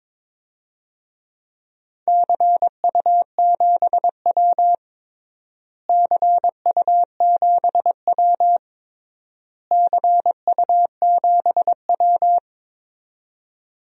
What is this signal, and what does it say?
Morse code, heard as a keyed tone, about 22 wpm, 700 Hz. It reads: CU7W CU7W CU7W
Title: DXCC-100 Call Signs - 4 Characters - Repeated 3x - 22wpm